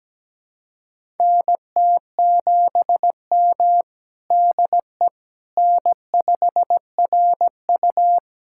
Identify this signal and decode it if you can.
Morse code, heard as a keyed tone, about 17 wpm, 700 Hz. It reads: NT7M DE N5RU
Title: USA Callsign DE Callsign - 4 Characters - 17wpm